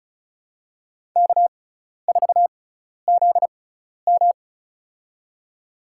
Morse code, heard as a keyed tone, about 34 wpm, spaced as for 12 wpm, 700 Hz. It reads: K4ZM
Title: USA Call Signs - 6x Character Spacing - 35wpm